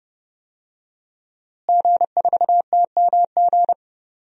Morse code, heard as a keyed tone, about 31 wpm, 700 Hz. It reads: G4TMG
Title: DXCC-100 Call Signs - 5 Characters - Repeated 2x - 30wpm